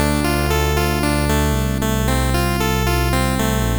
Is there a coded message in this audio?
No.